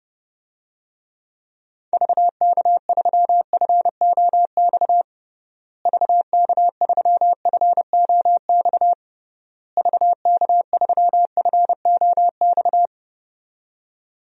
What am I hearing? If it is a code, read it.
Morse code, heard as a keyed tone, about 30 wpm, 700 Hz. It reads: VK3FOX VK3FOX VK3FOX
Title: DXCC-100 Call Signs - 6 Characters - Repeated 3x - 30wpm